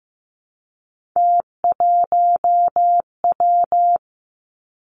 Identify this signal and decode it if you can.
Morse code, heard as a keyed tone, about 15 wpm, 700 Hz. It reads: T1W
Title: Three Challenging Letter-Number Combinations 15wpm